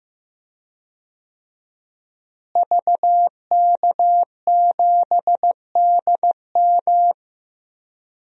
Morse code, heard as a keyed tone, about 15 wpm, 700 Hz. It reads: VK7DM
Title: DXCC-100 Call Signs - 5 Characters - Repeated 2x - 15wpm